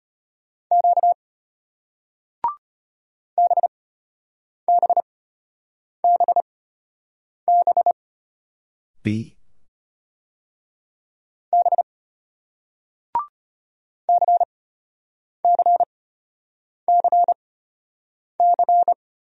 Morse code, heard as a keyed tone, about 36 wpm, spaced as for 8 wpm, 700 Hz. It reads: Q BBBB B CCCC